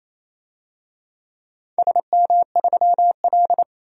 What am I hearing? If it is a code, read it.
Morse code, heard as a keyed tone, about 27 wpm, 700 Hz. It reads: SM3L
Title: DXCC-100 Call Signs - 4 Characters - 28wpm